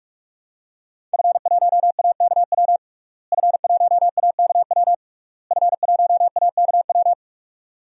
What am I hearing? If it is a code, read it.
Morse code, heard as a keyed tone, about 46 wpm, 700 Hz. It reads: F1AKW F1AKW F1AKW
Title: DXCC-100 Call Signs - 5 Characters - Repeated 3x - 45wpm